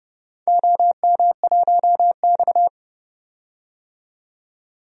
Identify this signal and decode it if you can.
Morse code, heard as a keyed tone, about 30 wpm, 700 Hz. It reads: OM1X
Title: DXCC-100 Call Signs - 4 Characters - 30wpm